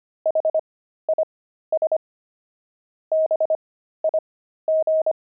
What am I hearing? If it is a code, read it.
Morse code, heard as a keyed tone, about 25 wpm, spaced as for 13 wpm, 630 Hz. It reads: HIS BIG
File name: pangram-fs-13-cs-25-explorer